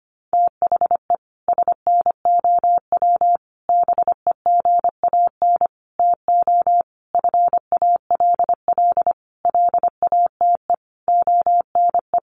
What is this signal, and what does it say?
Morse code, heard as a keyed tone, about 25 wpm, 700 Hz. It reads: THE SNOW BEGAN TO FALL LATE ONE
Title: Sentences from Top 1500 Words 25wpm